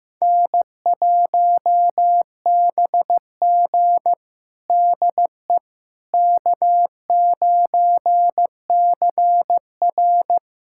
Morse code, heard as a keyed tone, about 15 wpm, 700 Hz. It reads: N1BG DE K9CR
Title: USA Callsign DE Callsign - 4 Characters - 15wpm